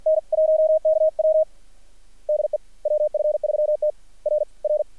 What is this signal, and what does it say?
Morse code, heard as a keyed tone, about 46 wpm, 610 Hz. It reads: N 1KW DE WF3T R R
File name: archived / CRSnet